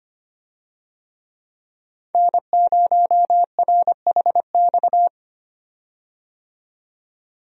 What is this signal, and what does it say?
Morse code, heard as a keyed tone, about 25 wpm, 700 Hz. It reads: N0RHX
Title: USA Call Signs - 5 Characters - 25wpm